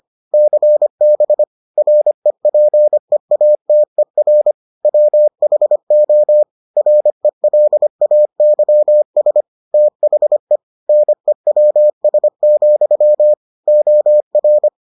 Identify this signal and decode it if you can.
Morse code, heard as a keyed tone, about 25 wpm, 600 Hz. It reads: CB REPEATER WHO RELAYS THE NEWS, OR